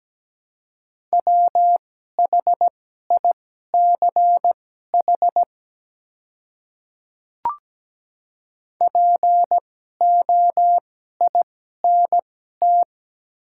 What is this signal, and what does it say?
Morse code, heard as a keyed tone, about 17 wpm, spaced as for 12 wpm, 700 Hz. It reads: WHICH POINT
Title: 2x Character Spacing - Top 200 Words - 17wpm